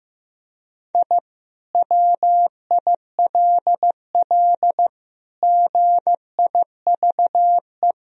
Morse code, heard as a keyed tone, about 15 wpm, 700 Hz. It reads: I WILL GIVE
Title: Sentences from Top 1000 Words - Review Part 4 - 15wpm